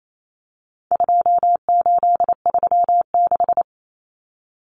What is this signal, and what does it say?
Morse code, heard as a keyed tone, about 28 wpm, 700 Hz. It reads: 2836
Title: Common Cut Numbers - 4-Digit 28wpm